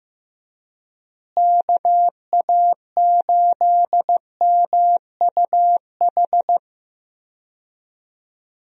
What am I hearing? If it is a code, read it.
Morse code, heard as a keyed tone, about 15 wpm, 700 Hz. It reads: KA8MUH
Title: USA Call Signs - 6 Characters - Repeated 3x - 15wpm